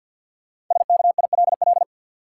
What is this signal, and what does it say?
Morse code, heard as a keyed tone, about 52 wpm, 700 Hz. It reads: SKILL
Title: Sentences from Top 1200 Words - New Word Drill - 50wpm